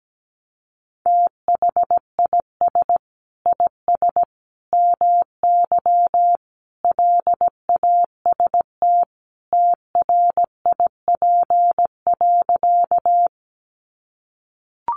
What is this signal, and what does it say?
Morse code, heard as a keyed tone, about 17 wpm, 700 Hz. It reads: THIS IS MY LAST TRIP.
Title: Sentences from Top 1300 Words 17wpm